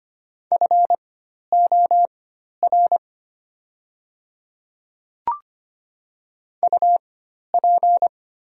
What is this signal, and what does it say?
Morse code, heard as a keyed tone, about 25 wpm, spaced as for 12 wpm, 700 Hz. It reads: FOR UP
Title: 4x Character Spacing - Top 100 Words - 25wpm